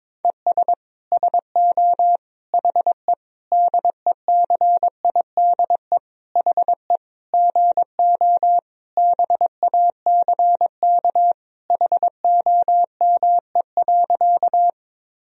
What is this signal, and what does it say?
Morse code, heard as a keyed tone, about 22 wpm, 700 Hz. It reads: ES SO HE DECIDE HE GO BACK HOME.
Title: Sentences from Top 500 Words and Abbreviations 22wpm